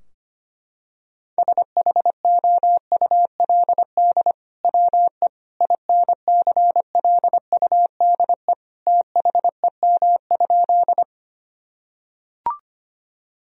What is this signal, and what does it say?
Morse code, heard as a keyed tone, about 25 wpm, 700 Hz. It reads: SHOULD WE INCLUDE THEM?
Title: Sentences from Top 1000 Words - Review Part 3 - 25wpm